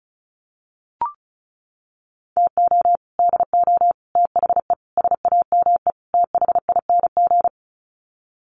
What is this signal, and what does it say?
Morse code, heard as a keyed tone, about 34 wpm, 700 Hz. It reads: TO DO THE SAME THING
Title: Sets of 5 Words Encore 35wpm